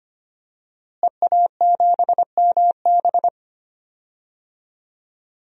Morse code, heard as a keyed tone, about 25 wpm, 700 Hz. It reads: EA7MB
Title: DXCC-100 Call Signs - 5 Characters - Repeated 2x - 25wpm